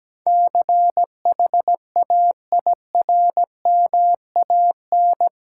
Morse code, heard as a keyed tone, about 17 wpm, 700 Hz. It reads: CHAIRMAN
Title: Sets of 3 Words - New Word Drill - 17wpm